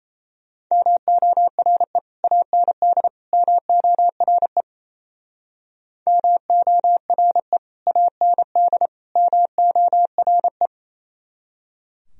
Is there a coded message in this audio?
Yes: Morse code from 0.7 to 10.7 s, heard as a keyed tone, about 37 wpm, 700 Hz. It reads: MORE AND MORE MORE AND MORE